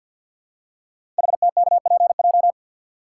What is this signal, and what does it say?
Morse code, heard as a keyed tone, about 50 wpm, 700 Hz. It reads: HTKPJ
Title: Code Groups - 5 Letter:Numbers - 50wpm